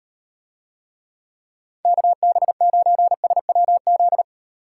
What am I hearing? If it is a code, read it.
Morse code, heard as a keyed tone, about 38 wpm, 700 Hz. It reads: KB9SWZ